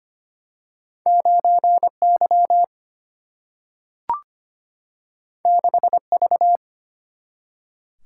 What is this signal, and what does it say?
Morse code, heard as a keyed tone, about 25 wpm, 700 Hz. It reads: 9 Y 6 V
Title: Single-Letter Single-Number 25wpm